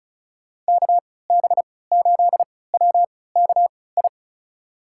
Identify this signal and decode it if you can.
Morse code, heard as a keyed tone, about 36 wpm, spaced as for 20 wpm, 700 Hz. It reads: KB8WKI
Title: USA Call Signs - 3x Character Spacing - 35wpm